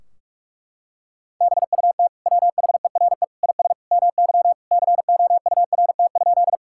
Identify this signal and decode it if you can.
Morse code, heard as a keyed tone, about 46 wpm, 700 Hz. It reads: BUT WHERE IS MY COURT?
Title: Sentences from Top 1600 Words 45wpm